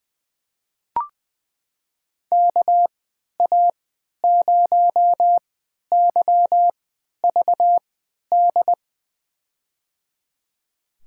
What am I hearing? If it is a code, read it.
Morse code, heard as a keyed tone, about 20 wpm, spaced as for 11 wpm, 700 Hz. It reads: KA0YVD
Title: USA Call Signs - 6 Characters - 3x Character Spacing - 20wpm